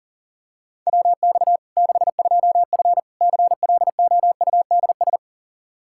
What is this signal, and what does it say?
Morse code, heard as a keyed tone, about 37 wpm, 700 Hz. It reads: WX 62F CLOUDS
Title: K1USN SST QSO Practice 40wpm